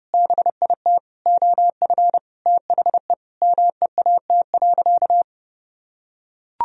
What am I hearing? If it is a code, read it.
Morse code, heard as a keyed tone, about 30 wpm, 700 Hz. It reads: BIT OF THE MEAT.
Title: Sentences from Top 1000 Words - Review Part 3 - 30wpm